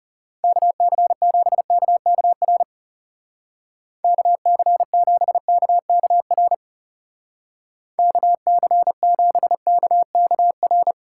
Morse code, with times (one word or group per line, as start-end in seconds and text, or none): 0.4-2.6 KC7KKR
4.0-6.5 KC7KKR
8.0-10.9 KC7KKR